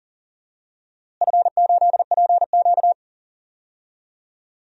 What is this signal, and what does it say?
Morse code, heard as a keyed tone, about 40 wpm, 700 Hz. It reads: F8PQ
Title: DXCC-100 Call Signs - 4 Characters - Repeated 3x - 40wpm